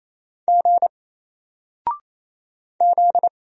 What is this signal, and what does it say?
Morse code, heard as a keyed tone, about 27 wpm, 700 Hz. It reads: G Z
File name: Single-Letter-Number-28wpm